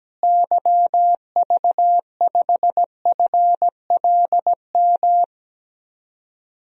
Morse code, heard as a keyed tone, about 17 wpm, 700 Hz. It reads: YV5FLM
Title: DXCC-100 Call Signs - 6 Characters - Repeated 3x - 17wpm